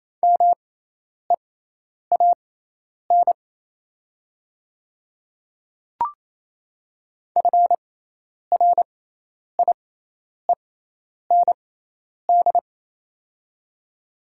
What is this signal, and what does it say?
Morse code, heard as a keyed tone, about 28 wpm, spaced as for 10 wpm, 700 Hz. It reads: MEAN FRIEND
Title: 6x Character Spacing - Top 300 Words - 28wpm